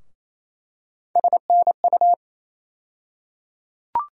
Morse code, heard as a keyed tone, about 27 wpm, 700 Hz. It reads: SNU
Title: USA Call Sign Postfixes - 3 Letters 28wpm